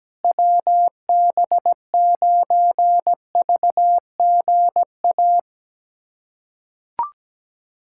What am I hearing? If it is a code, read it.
Morse code, heard as a keyed tone, about 17 wpm, 700 Hz. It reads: WB9VGA
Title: USA Callsign DE Callsign - 6 Characters - 17wpm